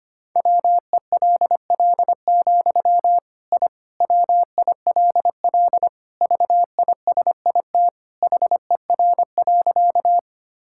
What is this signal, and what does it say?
Morse code, heard as a keyed tone, about 25 wpm, 700 Hz. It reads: WELL, I WILL VISIT HER.
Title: Sentences from Top 1000 Words - Review Part 6 - 25wpm